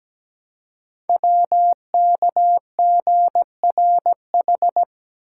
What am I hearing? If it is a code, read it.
Morse code, heard as a keyed tone, about 17 wpm, 700 Hz. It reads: WKGRH
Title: Code Groups - 5 Letters - 17wpm